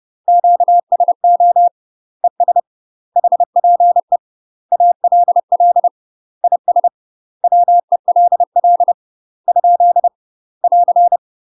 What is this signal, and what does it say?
Morse code, heard as a keyed tone, about 30 wpm, 700 Hz. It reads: QSO ES HPE ALL IS WELL ? <AR>
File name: UK MASTER QSOs TRAINING / 30WPM / UK MASTER QSOs 0041